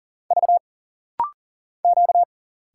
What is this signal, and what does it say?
Morse code, heard as a keyed tone, about 40 wpm, 700 Hz. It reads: V Q
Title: Single Letters - Rapid-Fire 40wpm